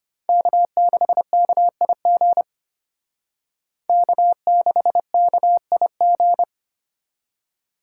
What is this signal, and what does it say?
Morse code, heard as a keyed tone, about 31 wpm, 700 Hz. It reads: K6KIG K6KIG
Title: Speed Racing - USA 1x3 Call Signs 25wpm